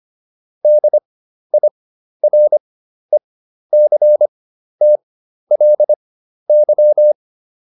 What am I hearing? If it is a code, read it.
Morse code, heard as a keyed tone, about 25 wpm, spaced as for 12 wpm, 600 Hz. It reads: DIRECTLY